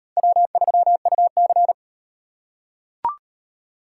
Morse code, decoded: W3UC

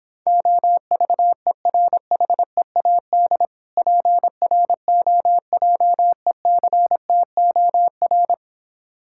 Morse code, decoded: OVERHEAD PROJECTOR

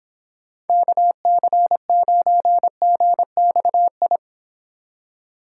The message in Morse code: KC9GXI

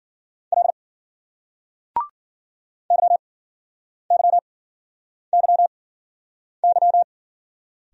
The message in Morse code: L YYYY

